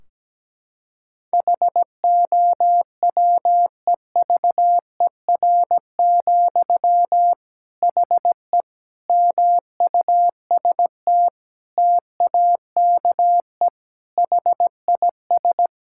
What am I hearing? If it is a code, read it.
Morse code, heard as a keyed tone, about 17 wpm, 700 Hz. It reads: HOWEVER, HE MUST TAKE HIS
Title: Sentences from Top 1600 Words 17wpm